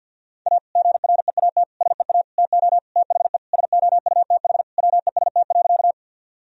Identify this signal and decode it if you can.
Morse code, heard as a keyed tone, about 52 wpm, 700 Hz. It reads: A GREAT SEA TO THE SOUTH WEST.
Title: Sentences from Top 1000 Words - Review Part 2 - 50wpm